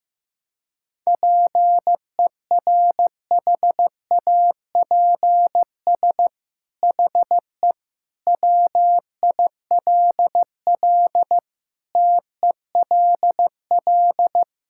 Morse code, heard as a keyed tone, about 16 wpm, 700 Hz. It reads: PERHAPS HE WILL TELL
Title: Sentences from Top 1600 Words 15wpm